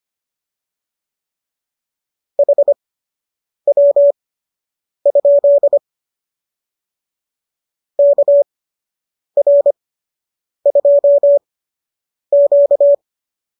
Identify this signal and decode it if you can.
Morse code, heard as a keyed tone, about 25 wpm, spaced as for 8 wpm, 590 Hz. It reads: HW? KR2Q